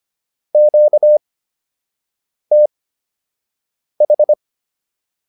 Morse code, heard as a keyed tone, about 25 wpm, spaced as for 6 wpm, 600 Hz. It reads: QTH
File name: QSO 203 06